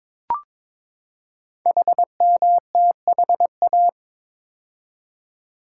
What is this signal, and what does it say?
Morse code, heard as a keyed tone, about 22 wpm, 700 Hz. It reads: HMTHA
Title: Code Groups - 5 Letters - 22wpm